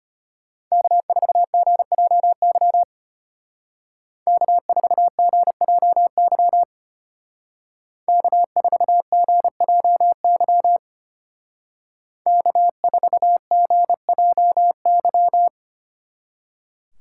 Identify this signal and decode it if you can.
Morse code, heard as a keyed tone, about 30 wpm, 700 Hz. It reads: K4GJY K4GJY K4GJY K4GJY
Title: Speed Racing - USA 1x3 Call Signs 25wpm